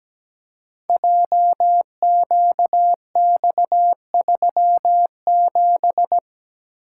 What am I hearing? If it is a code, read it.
Morse code, heard as a keyed tone, about 17 wpm, 700 Hz. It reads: JQX37
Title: Code Groups - 5 Letter:Numbers - 17wpm